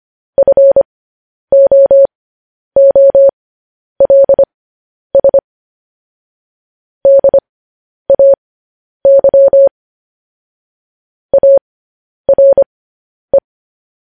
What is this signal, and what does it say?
Morse code, heard as a keyed tone, about 25 wpm, spaced as for 10 wpm, 550 Hz. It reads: FOOLS DAY ARE
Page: News Headlines in Morse Code at 10 WPM